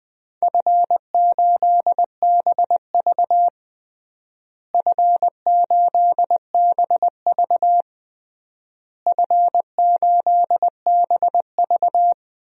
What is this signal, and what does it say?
Morse code, heard as a keyed tone, about 20 wpm, 700 Hz. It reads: F8BV F8BV F8BV
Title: DXCC-100 Call Signs - 4 Characters - Repeated 3x - 20wpm